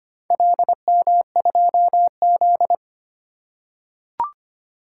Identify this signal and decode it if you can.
Morse code, heard as a keyed tone, about 25 wpm, 700 Hz. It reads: LM2Z
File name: practice / 25wpm / Four Letter-Number Combinations - 25wpm